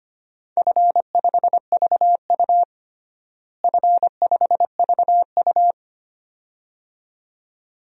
Morse code, heard as a keyed tone, about 25 wpm, 700 Hz. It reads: F5VU F5VU
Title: DXCC-100 Call Signs - 4 Characters - Repeated 2x - 25wpm